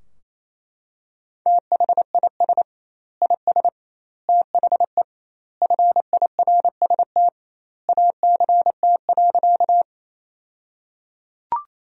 Morse code, heard as a keyed tone, about 28 wpm, 700 Hz. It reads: THIS IS THE FIRST ACT.
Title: 2x Word-Spacing - Sentences from Top 200 Words - 28wpm